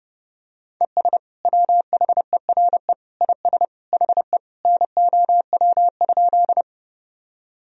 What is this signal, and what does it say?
Morse code, heard as a keyed tone, about 30 wpm, 700 Hz. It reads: ES WHERE IS HE NOW?
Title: Sentences from Top 200 Words and Abbreviations 30wpm